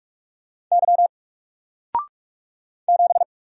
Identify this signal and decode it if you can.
Morse code, heard as a keyed tone, about 44 wpm, 700 Hz. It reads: Y 7
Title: Single Letter-Number Alt 45wpm